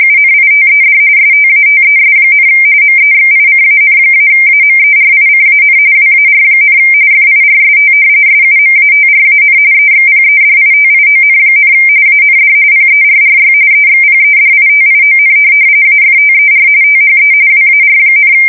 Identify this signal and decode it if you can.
RTTY, about 45.45 baud. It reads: UNITED STATES, MEXICAN-AMERICAN WAR, FIRST SUMATRAN EXPEDITION, AND PERRY EXPEDITION AFTER THE WAR,
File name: fsk170 / usnavy2